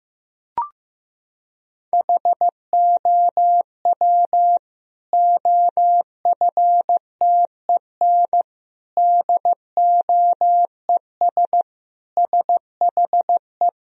Morse code, heard as a keyed tone, about 15 wpm, 700 Hz. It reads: HOW OFTEN DOES SHE
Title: Sentences from Top 1000 Words - Review Part 3 - 15wpm